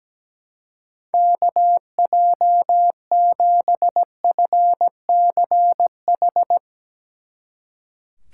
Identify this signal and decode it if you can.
Morse code, heard as a keyed tone, about 17 wpm, 700 Hz. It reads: KJ7FCH
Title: Speed Racing - USA 2x3 Call Signs 17wpm